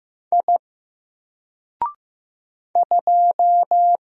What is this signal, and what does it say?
Morse code, heard as a keyed tone, about 15 wpm, 700 Hz. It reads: I 2
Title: Single Letter-Number Alt 15wpm